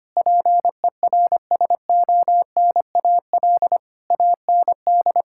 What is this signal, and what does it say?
Morse code, heard as a keyed tone, about 25 wpm, 700 Hz. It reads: PERSONAL AND
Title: Sentences from Top 1000 Words - Review Part 5 - 25wpm